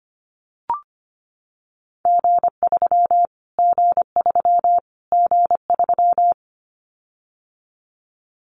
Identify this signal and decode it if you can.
Morse code, heard as a keyed tone, about 25 wpm, 700 Hz. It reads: G3 G3 G3
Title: Callsign Prefixes for Top 25 DXCC - Repeated 3x - 25wpm